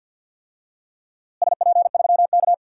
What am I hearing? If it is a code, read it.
Morse code, heard as a keyed tone, about 49 wpm, 700 Hz. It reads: SP3K
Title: DXCC-100 Call Signs - 4 Characters - Repeated 3x - 50wpm